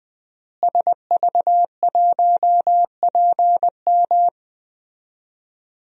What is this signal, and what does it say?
Morse code, heard as a keyed tone, about 21 wpm, 700 Hz. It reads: SV1PM
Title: DXCC-100 Call Signs - 5 Characters - Repeated 3x - 20wpm